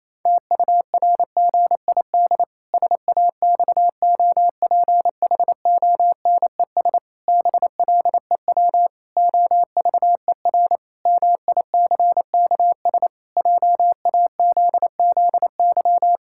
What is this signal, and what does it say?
Morse code, heard as a keyed tone, about 27 wpm, 700 Hz. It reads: TURGID SAXOPHONES BLEW OVER MICKS JAZZY
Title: Pangram Challenge v2 - 28wpm